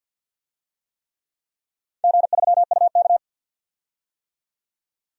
Morse code, heard as a keyed tone, about 53 wpm, 700 Hz. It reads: G3UK
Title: DXCC-100 Call Signs - 4 Characters - 50wpm